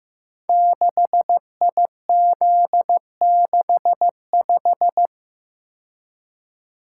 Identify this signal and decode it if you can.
Morse code, heard as a keyed tone, about 15 wpm, 700 Hz. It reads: 6IZ65
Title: Code Groups - 5 Letter:Numbers - 15wpm